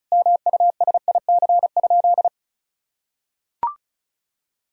Morse code, heard as a keyed tone, about 34 wpm, 700 Hz. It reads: MUSIC?
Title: Sentences from Top 1000 Words - Review Part 1 - 35wpm